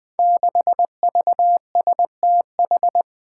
Morse code, heard as a keyed tone, about 20 wpm, 700 Hz. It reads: 6VSTH
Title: Five Challenging Letter-Number Combinations 20wpm